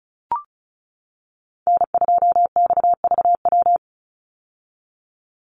Morse code, heard as a keyed tone, about 34 wpm, 700 Hz. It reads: N2XVW
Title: USA Call Signs 35wpm